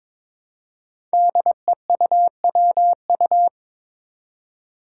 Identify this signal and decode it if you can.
Morse code, heard as a keyed tone, about 22 wpm, 700 Hz. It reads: DEUWU
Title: Five Challenging Letter-Number Combinations 22wpm